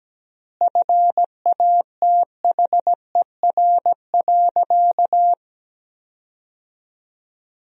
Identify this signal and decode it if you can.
Morse code, heard as a keyed tone, about 17 wpm, 700 Hz. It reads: FATHER.
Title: 3x Word-Spacing - Sentences from Top 300 Words - 17wpm